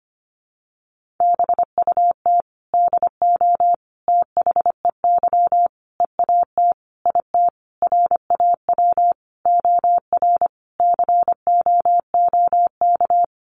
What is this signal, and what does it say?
Morse code, heard as a keyed tone, about 25 wpm, 700 Hz. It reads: BUT DO THEY EAT IT RAW OR COOK
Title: Sentences from Top 1500 Words 25wpm